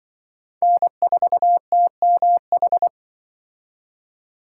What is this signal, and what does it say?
Morse code, heard as a keyed tone, about 24 wpm, 700 Hz. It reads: N4TMH
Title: Speed Racing - USA Call Signs 20wpm